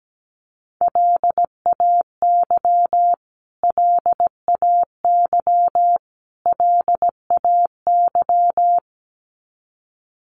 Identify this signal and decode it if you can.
Morse code, heard as a keyed tone, about 17 wpm, 700 Hz. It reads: LAY LAY LAY
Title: Top 500 Words Repeated 17wpm